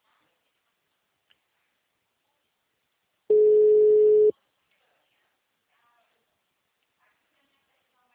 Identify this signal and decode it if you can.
Ringback tone (Europe, 425 Hz), with speech.
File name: out- -2030- - - .53001